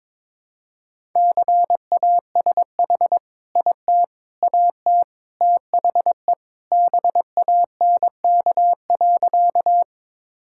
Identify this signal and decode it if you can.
Morse code, heard as a keyed tone, about 22 wpm, 700 Hz. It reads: CASH IT AT THE BANK.
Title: Sentences from Top 1000 Words - Review Part 5 - 22wpm